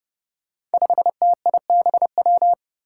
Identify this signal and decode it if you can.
Morse code, heard as a keyed tone, about 29 wpm, 700 Hz. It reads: 5TIBW
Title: Five Challenging Letter-Number Combinations 30wpm